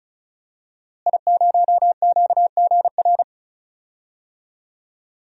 Morse code, heard as a keyed tone, about 36 wpm, 700 Hz. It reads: I0QGR
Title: DXCC-100 Call Signs - 5 Characters - Repeated 2x - 35wpm